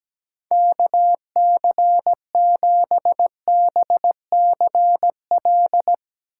Morse code, heard as a keyed tone, about 17 wpm, 700 Hz. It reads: KC7BCL
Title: USA Call Signs - 3x Character Spacing - 17wpm